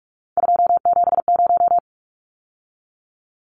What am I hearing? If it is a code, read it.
Morse code, heard as a keyed tone, about 44 wpm, 700 Hz. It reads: 270
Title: Numbers - 3-Digit 45wpm